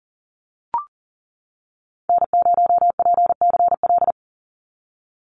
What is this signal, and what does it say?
Morse code, heard as a keyed tone, about 40 wpm, 700 Hz. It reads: N0PCL
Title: NA SOTA Activator Callsigns 40wpm